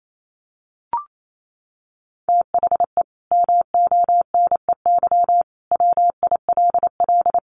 Morse code, heard as a keyed tone, about 28 wpm, 700 Hz. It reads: THE MONEY WILL
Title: Sentences from Top 1500 Words 28wpm